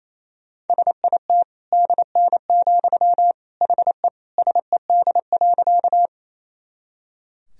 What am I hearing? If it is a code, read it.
Morse code, heard as a keyed tone, about 28 wpm, 700 Hz. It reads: SIT DN, HE SED.
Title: Sentences from Top 500 Words and Abbreviations 28wpm